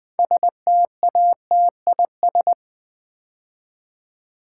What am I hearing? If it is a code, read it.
Morse code, heard as a keyed tone, about 20 wpm, 700 Hz. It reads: STATIS
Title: Build A Word - 3 Letters At A Time - 9-Letter Words - 20wpm